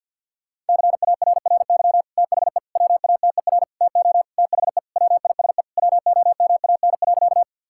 Morse code, heard as a keyed tone, about 48 wpm, 700 Hz. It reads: CARRY THE WATER TO THE WISE WOMAN.
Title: Sentences from Top 1600 Words 50wpm